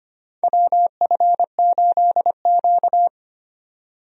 Morse code, heard as a keyed tone, about 25 wpm, 700 Hz. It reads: WF8Q
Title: USA Call Signs - 4 Characters - 25wpm